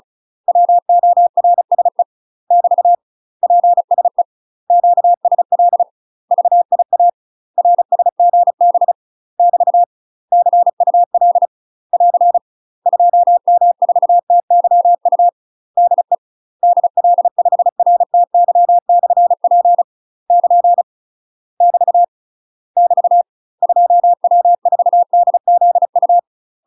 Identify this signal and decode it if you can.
Morse code, heard as a keyed tone, about 36 wpm, 700 Hz. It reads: WORSE <BT> PSE QSL VIA RSGB <BT> CUL <AR> 2M4TYU DE DL5RTY/P <KN> <BT> <BT> 2W4DZU
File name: UK MASTER QSOs TRAINING / 35WPM / UK MASTER QSOs 0002